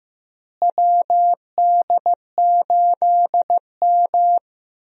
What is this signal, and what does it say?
Morse code, heard as a keyed tone, about 15 wpm, 700 Hz. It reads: WD8M
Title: USA Call Signs - 4 Characters - Repeated 3x - 15wpm